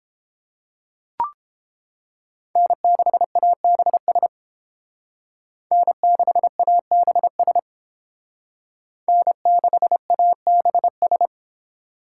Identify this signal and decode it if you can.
Morse code, heard as a keyed tone, about 31 wpm, 700 Hz. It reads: N6ABS N6ABS N6ABS